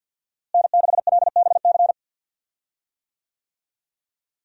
Morse code, heard as a keyed tone, about 52 wpm, 700 Hz. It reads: N6LBC
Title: USA Call Signs - 5 Characters - Repeated 3x - 50wpm